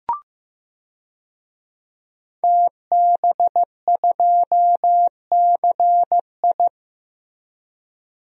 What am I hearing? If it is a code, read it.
Morse code, heard as a keyed tone, about 15 wpm, 700 Hz. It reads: TB2CI
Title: DXCC-100 Call Signs - 5 Characters - Repeated 2x - 15wpm